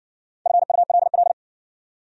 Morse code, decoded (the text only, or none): FULL